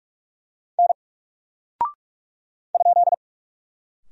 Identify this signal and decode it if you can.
Morse code, heard as a keyed tone, about 43 wpm, 700 Hz. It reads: N ?
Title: Single Character (Letter:Number:Punctuation:Prosign) Alt 45wpm